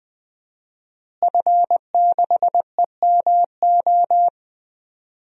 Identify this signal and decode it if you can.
Morse code, heard as a keyed tone, about 20 wpm, 700 Hz. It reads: F6EMO
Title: DXCC-100 Call Signs - 5 Characters - Repeated 3x - 20wpm